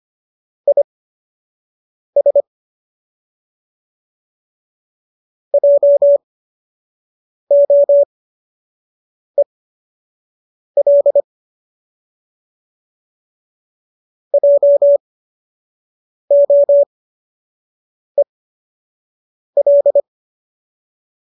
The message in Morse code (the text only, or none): IS JOEL JOEL